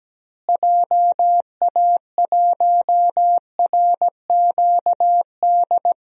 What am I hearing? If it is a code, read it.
Morse code, heard as a keyed tone, about 17 wpm, 700 Hz. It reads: JA1RQD
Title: DXCC-100 Call Signs - 6 Characters - Repeated 3x - 17wpm